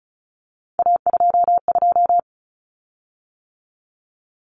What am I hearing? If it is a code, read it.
Morse code, heard as a keyed tone, about 34 wpm, 700 Hz. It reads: A22
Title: Common Cut Numbers - 3-Digit 35wpm